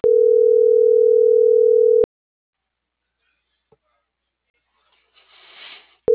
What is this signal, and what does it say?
Ringback tone (North America), 440 + 480 Hz.